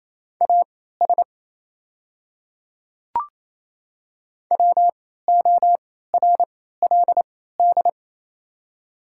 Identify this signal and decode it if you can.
Morse code, heard as a keyed tone, about 27 wpm, spaced as for 16 wpm, 690 Hz. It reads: AS WORLD
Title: 3x Character Spacing - Top 200 Words - 28wpm